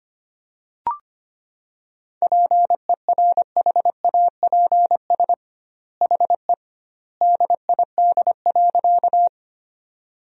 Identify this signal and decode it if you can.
Morse code, heard as a keyed tone, about 25 wpm, 700 Hz. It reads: PERHAPS HE DID.
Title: 2x Word-Spacing - Sentences from Top 500 Words - 25wpm